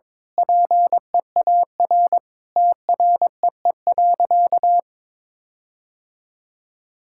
Morse code, heard as a keyed tone, about 22 wpm, 700 Hz. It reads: PEAR TREE.
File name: Sentences with 9-Word Phrases - 22wpm